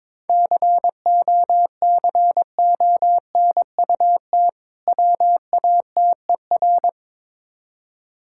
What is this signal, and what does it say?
Morse code, heard as a keyed tone, about 22 wpm, 690 Hz. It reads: COCONUT WATER
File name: Difficult Words with Hints - 22wpm